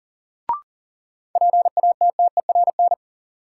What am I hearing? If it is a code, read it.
Morse code, heard as a keyed tone, about 37 wpm, 700 Hz. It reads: PATTERN
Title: Sentences from Top 1000 Words - New Word Drill - 40wpm